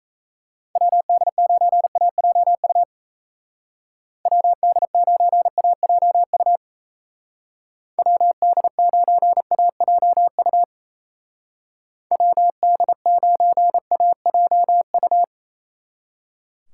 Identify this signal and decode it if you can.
Morse code, heard as a keyed tone, about 35 wpm, 700 Hz. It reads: WD9AJU WD9AJU WD9AJU WD9AJU